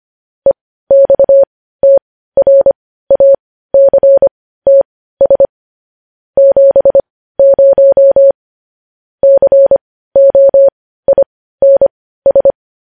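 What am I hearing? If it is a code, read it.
Morse code, heard as a keyed tone, about 25 wpm, spaced as for 15 wpm, 550 Hz. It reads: EXTRACTS 70 COINS
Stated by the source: Morse code news headlines broadcast.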